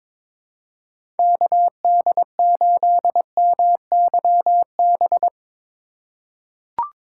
Morse code, heard as a keyed tone, about 22 wpm, 700 Hz. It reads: KD8MYB
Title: USA Call Signs - 3x Character Spacing - 22wpm